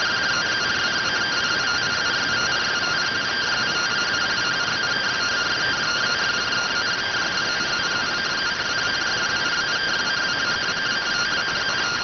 Two-tone teleprinter signal, about 170 Hz shift.